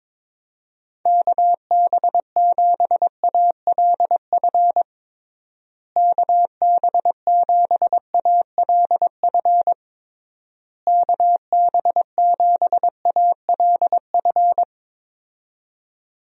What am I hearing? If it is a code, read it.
Morse code, heard as a keyed tone, about 22 wpm, 700 Hz. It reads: KB7ALF KB7ALF KB7ALF
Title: USA Call Signs - 6 Characters - Repeated 3x - 22wpm